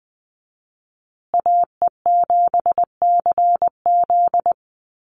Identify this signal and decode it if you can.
Morse code, heard as a keyed tone, about 20 wpm, 700 Hz. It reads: AE7CZ
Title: USA Call Signs Repeated 3x 20wpm